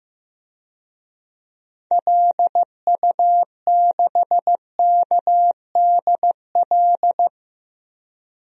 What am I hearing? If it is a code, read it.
Morse code, heard as a keyed tone, about 15 wpm, 700 Hz. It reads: LU6KDL
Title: DXCC-100 Call Signs - 6 Characters - Repeated 3x - 15wpm